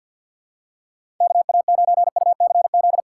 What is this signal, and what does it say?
Morse code, heard as a keyed tone, about 52 wpm, 700 Hz. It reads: KA9UCZ
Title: USA Callsign DE Callsign - 6 Characters - 50wpm